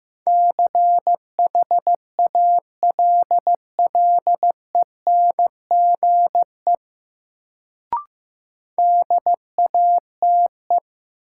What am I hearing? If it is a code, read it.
Morse code, heard as a keyed tone, about 15 wpm, 700 Hz. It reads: CHALLENGE DATE
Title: Sentences from Top 1000 Words - Review Part 7 - New Word Drill - 15wpm